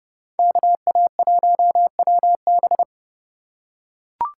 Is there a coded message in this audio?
Yes: Morse code from 0.4 to 2.8 s, heard as a keyed tone, about 30 wpm, 700 Hz. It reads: KA1WB